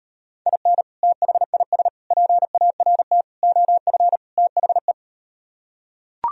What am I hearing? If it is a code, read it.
Morse code, heard as a keyed tone, about 38 wpm, 700 Hz. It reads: IN THIS PART OF THE